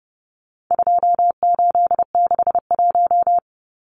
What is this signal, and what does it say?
Morse code, heard as a keyed tone, about 30 wpm, 700 Hz. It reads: 2861
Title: Numbers - 4-Digit 30wpm